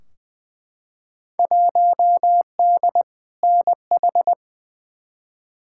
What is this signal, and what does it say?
Morse code, heard as a keyed tone, about 21 wpm, 700 Hz. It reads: 1D NH
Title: Field Day Exchanges - 20wpm